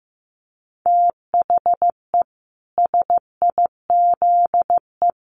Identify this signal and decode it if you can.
Morse code, heard as a keyed tone, about 15 wpm, 700 Hz. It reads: THE SIZE
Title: Sets of 4 Words 15wpm